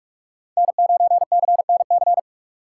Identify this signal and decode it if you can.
Morse code, heard as a keyed tone, about 47 wpm, 700 Hz. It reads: N9CNC
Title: USA Call Signs - 5 Characters - 6x Character Spacing - 45wpm